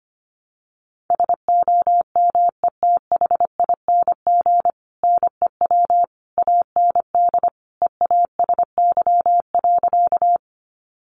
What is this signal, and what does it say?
Morse code, heard as a keyed tone, about 25 wpm, 700 Hz. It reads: SOMETHING NEW AND EASY.
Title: Sentences from Top 700 Words 25wpm